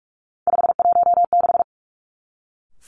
Morse code, heard as a keyed tone, about 46 wpm, 700 Hz. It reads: 516
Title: Three Challenging Letter-Number Combinations 45wpm